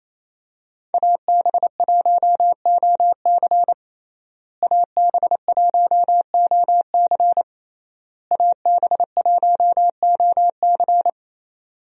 Morse code, heard as a keyed tone, about 28 wpm, 700 Hz. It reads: AB1OC AB1OC AB1OC
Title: USA Call Signs - 5 Characters - Repeated 3x - 28wpm